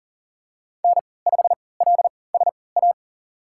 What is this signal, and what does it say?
Morse code, heard as a keyed tone, about 40 wpm, spaced as for 23 wpm, 700 Hz. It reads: N5LSA
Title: USA Call Signs - 5 Characters - 3x Character Spacing - 40wpm